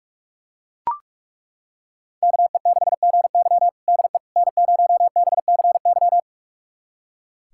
Morse code, heard as a keyed tone, about 42 wpm, 700 Hz. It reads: KE6GY DE N0BCY
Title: USA Callsign DE Callsign 45wpm